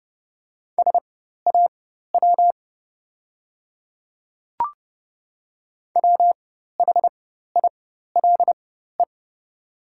Morse code, heard as a keyed tone, about 30 wpm, spaced as for 14 wpm, 700 Hz. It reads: SAW WHILE
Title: 4x Character Spacing - Top 300 Words - 30wpm